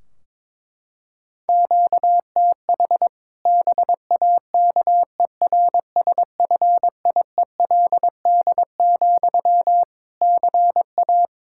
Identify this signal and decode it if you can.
Morse code, heard as a keyed tone, about 22 wpm, 700 Hz. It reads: QTH BAKERSFIELD, CA
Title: Practice QSO - Exchange 1 - Beginning - 22wpm